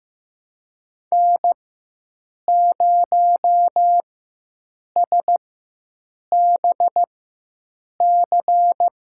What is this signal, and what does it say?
Morse code, heard as a keyed tone, about 16 wpm, spaced as for 7 wpm, 700 Hz. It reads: N0SBC
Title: USA Call Signs - 5 Characters - 4x Character Spacing - 15wpm